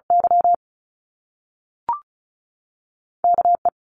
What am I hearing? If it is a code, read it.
Morse code, heard as a keyed tone, about 36 wpm, 700 Hz. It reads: Y K E
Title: Top Two Letter Combinations in Words 35wpm